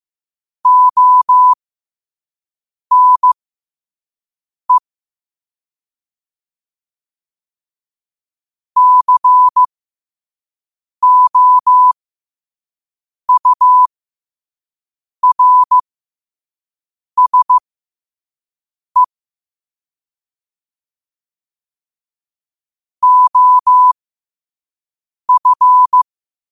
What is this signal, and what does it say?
Morse code, heard as a keyed tone, about 15 wpm, spaced as for 5 wpm, 1000 Hz. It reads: ONE COURSE OF
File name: qotdfiles / 5wpm